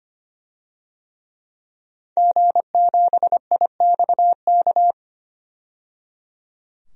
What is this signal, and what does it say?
Morse code, heard as a keyed tone, about 25 wpm, 700 Hz. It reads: G7IXK
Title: DXCC-100 Call Signs - 5 Characters - 25wpm